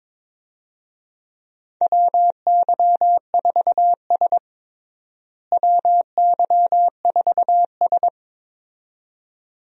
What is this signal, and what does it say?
Morse code, heard as a keyed tone, about 22 wpm, 700 Hz. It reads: WY4S WY4S
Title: USA Call Signs - 4 Characters - Repeated 2x - 22wpm